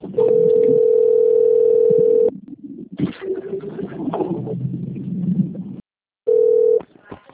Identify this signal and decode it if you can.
Ringback tone (North America), 440 + 480 Hz.